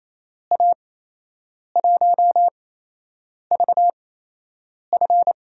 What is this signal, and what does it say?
Morse code, heard as a keyed tone, about 28 wpm, spaced as for 8 wpm, 700 Hz. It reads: A1VF
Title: USA Call Signs - 6 Characters - 8x Character Spacing - 28wpm